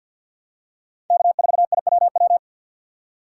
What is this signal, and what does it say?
Morse code, heard as a keyed tone, about 49 wpm, 700 Hz. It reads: K4IWW
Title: USA Call Signs - 5 Characters - 6x Character Spacing - 50wpm